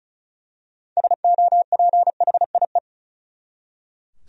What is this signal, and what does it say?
Morse code, heard as a keyed tone, about 36 wpm, 700 Hz. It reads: SOPHIE
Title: USA Common Names - 35wpm